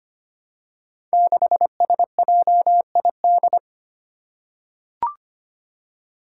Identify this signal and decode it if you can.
Morse code, heard as a keyed tone, about 25 wpm, 700 Hz. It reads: 6SJID
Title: Five Challenging Letter-Number Combinations 25wpm